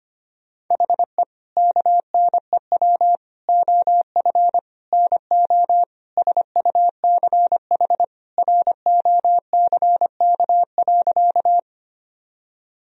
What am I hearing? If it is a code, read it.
Morse code, heard as a keyed tone, about 25 wpm, 700 Hz. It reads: HE KNEW OF NO SUCH ROCK.
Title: Sentences from Top 1000 Words - Review Part 2 - 25wpm